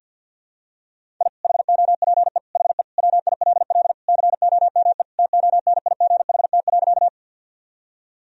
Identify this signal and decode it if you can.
Morse code, heard as a keyed tone, about 53 wpm, 700 Hz. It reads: I HOPE HE WILL COME TONIGHT.
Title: Sentences from Top 1000 Words - Review Part 3 - 50wpm